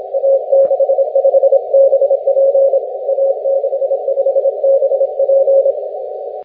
Morse code, heard as a keyed tone, about 26 wpm, 540 Hz. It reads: A65BP A65BP
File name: wp / a65bp 6m cw 02 jul 2011